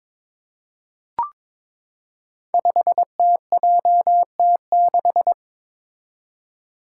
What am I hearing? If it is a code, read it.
Morse code, heard as a keyed tone, about 22 wpm, 700 Hz. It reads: 5TJT6
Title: Five Challenging Letter-Number Combinations 22wpm